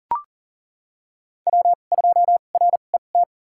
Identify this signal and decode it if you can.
Morse code, heard as a keyed tone, about 38 wpm, spaced as for 28 wpm, 700 Hz. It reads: W2RET
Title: USA Call Signs - 5 Characters - 2x Character Spacing - 40wpm